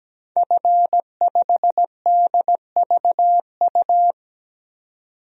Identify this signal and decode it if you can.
Morse code, heard as a keyed tone, about 17 wpm, 700 Hz. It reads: F5DVU
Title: DXCC-100 Call Signs - 5 Characters - 17wpm